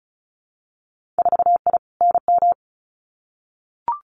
Morse code, heard as a keyed tone, about 36 wpm, 700 Hz. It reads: 4I NM
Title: Winter Field Day Exchanges Encore 35wpm